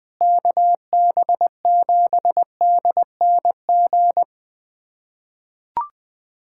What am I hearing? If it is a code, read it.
Morse code, heard as a keyed tone, about 21 wpm, 700 Hz. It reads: KB7DNG
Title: USA Callsign DE Callsign - 6 Characters - 20wpm